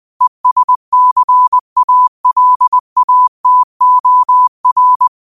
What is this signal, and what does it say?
Morse code, heard as a keyed tone, about 20 wpm, 1000 Hz. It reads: ESCALATOR
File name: qotdfiles / 20wpm